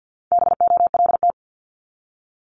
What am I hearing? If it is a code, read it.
Morse code, heard as a keyed tone, about 50 wpm, 700 Hz. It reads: BOLT
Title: Multi-letter Phonemes 50wpm